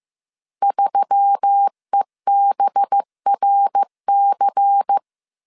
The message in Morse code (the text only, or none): 3EBRC